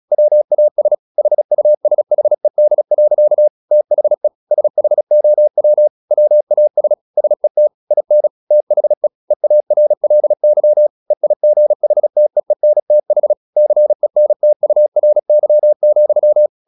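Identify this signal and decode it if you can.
Morse code, heard as a keyed tone, about 36 wpm, 600 Hz. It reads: WAS HUSHED. THE SHOW WAS SET IN THE EARLY EIGHTEENTH CENTURY,
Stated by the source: Morse practice at 36 wpm, text of Ken Follett's novel Never, from CW med Gnister.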